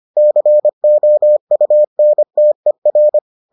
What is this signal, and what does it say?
Morse code, heard as a keyed tone, about 25 wpm, 600 Hz. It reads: COUNTER